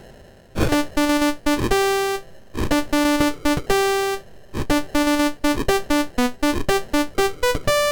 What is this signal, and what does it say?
There is no code.